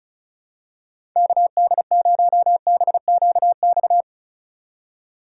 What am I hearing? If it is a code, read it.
Morse code, heard as a keyed tone, about 34 wpm, 700 Hz. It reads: KD0BQX
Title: USA Call Signs - 6 Characters - 2x Character Spacing - 35wpm